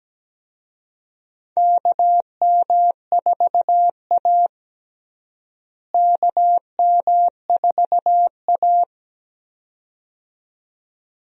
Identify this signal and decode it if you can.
Morse code, heard as a keyed tone, about 17 wpm, 700 Hz. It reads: KM4A KM4A
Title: USA Call Signs - 4 Characters - Repeated 2x - 17wpm